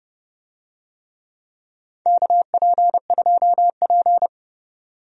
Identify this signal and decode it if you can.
Morse code, heard as a keyed tone, about 30 wpm, 700 Hz. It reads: KP2P
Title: DXCC-100 Call Signs - 4 Characters - 30wpm